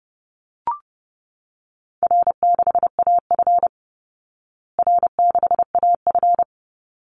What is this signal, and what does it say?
Morse code, heard as a keyed tone, about 30 wpm, 700 Hz. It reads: R6AF R6AF
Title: Easy Call Signs Repeated 2x - 30wpm